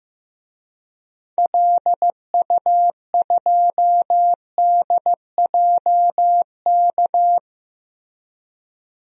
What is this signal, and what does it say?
Morse code, heard as a keyed tone, about 15 wpm, 700 Hz. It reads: LU2DJK
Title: DXCC-100 Call Signs - 6 Characters - Repeated 3x - 15wpm